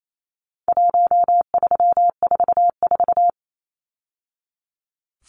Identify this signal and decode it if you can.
Morse code, heard as a keyed tone, about 28 wpm, 700 Hz. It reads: 1344
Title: Numbers - 4-Digit 28wpm